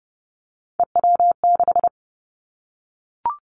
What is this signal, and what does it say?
Morse code, heard as a keyed tone, about 30 wpm, 700 Hz. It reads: EW6
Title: Three Challenging Letter-Number Combinations 30wpm